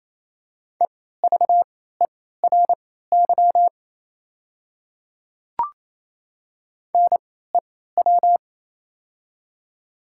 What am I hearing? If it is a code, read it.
Morse code, heard as a keyed tone, about 28 wpm, spaced as for 16 wpm, 700 Hz. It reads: EVERY NEW
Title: 3x Character Spacing - Top 200 Words - 28wpm